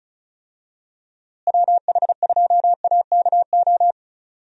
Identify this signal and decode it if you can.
Morse code, heard as a keyed tone, about 34 wpm, 700 Hz. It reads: WH2AKO
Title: USA Call Signs - 6 Characters - Repeated 2x - 35wpm